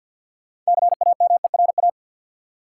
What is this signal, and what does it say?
Morse code, heard as a keyed tone, about 51 wpm, 700 Hz. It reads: CAMERA
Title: Sentences from Top 1000 Words - Review Part 5 - New Word Drill - 50wpm